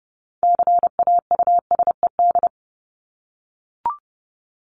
Morse code, heard as a keyed tone, about 30 wpm, 700 Hz. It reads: CAUSED
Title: ED Suffix - Top 100 Words with Base Words 30wpm